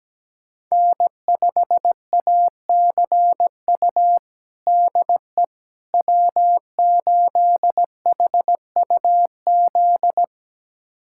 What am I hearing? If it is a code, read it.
Morse code, heard as a keyed tone, about 17 wpm, 700 Hz. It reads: N5ACU DE W8HUZ
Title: USA Callsign DE Callsign - 5 Characters - 17wpm